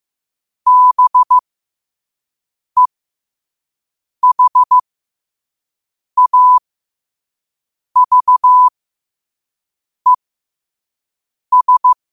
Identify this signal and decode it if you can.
Morse code, heard as a keyed tone, about 15 wpm, spaced as for 5 wpm, 1000 Hz. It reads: BEHAVES
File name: qotdfiles / 5wpm